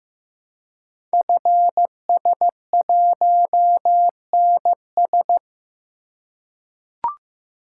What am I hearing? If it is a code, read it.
Morse code, heard as a keyed tone, about 15 wpm, 700 Hz. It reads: FS1NS
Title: Code Groups - 5 Letter:Numbers - 15wpm